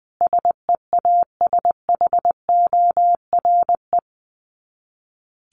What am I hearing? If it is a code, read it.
Morse code, heard as a keyed tone, about 20 wpm, 700 Hz. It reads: SEASHORE
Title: Dit Challenge 20wpm